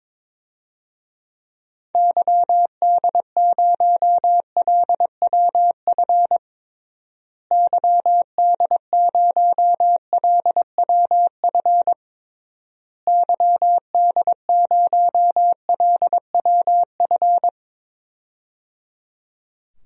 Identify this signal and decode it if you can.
Morse code, heard as a keyed tone, about 22 wpm, 700 Hz. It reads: YD0LWF YD0LWF YD0LWF
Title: DXCC-100 Call Signs - 6 Characters - Repeated 3x - 22wpm